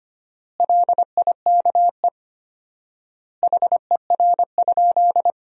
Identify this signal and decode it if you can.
Morse code, heard as a keyed tone, about 25 wpm, 700 Hz. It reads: LIKE HER?
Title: 4x Word-Spacing - Sentences from Top 100 Words - 25wpm